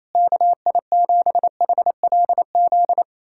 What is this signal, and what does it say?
Morse code, heard as a keyed tone, about 28 wpm, 700 Hz. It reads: KI7HLZ
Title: USA Callsign DE Callsign - 6 Characters - 28wpm